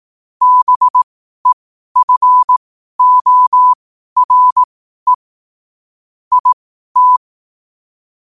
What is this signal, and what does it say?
Morse code, heard as a keyed tone, about 18 wpm, spaced as for 12 wpm, 1000 Hz. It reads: BEFORE IT
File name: qotdfiles / 12wpm